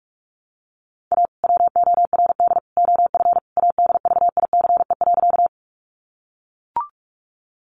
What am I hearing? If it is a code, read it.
Morse code, heard as a keyed tone, about 44 wpm, 700 Hz. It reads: A WORD OF ADVICE.
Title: Sentences from Top 1100 Words 45wpm